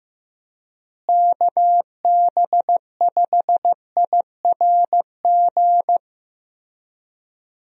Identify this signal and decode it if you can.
Morse code, heard as a keyed tone, about 15 wpm, 700 Hz. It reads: KB5IRG
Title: USA Call Signs - 6 Characters - Repeated 3x - 15wpm